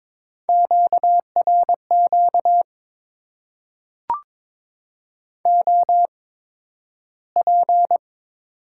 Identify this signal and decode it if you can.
Morse code, heard as a keyed tone, about 22 wpm, spaced as for 12 wpm, 690 Hz. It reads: QRQ OP
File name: Top 90 QSO Elements - 8x Character Spacing - 22wpm